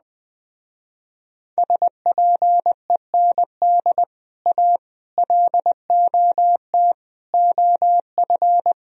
Morse code, heard as a keyed tone, about 20 wpm, 700 Hz. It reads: SPEND A LOT OF
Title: Speed-Racing - Sets of 5 Words 17wpm